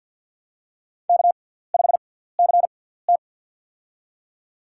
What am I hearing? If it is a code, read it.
Morse code, heard as a keyed tone, about 53 wpm, spaced as for 17 wpm, 700 Hz. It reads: K5CT
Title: USA Call Signs - 6x Character Spacing - 50wpm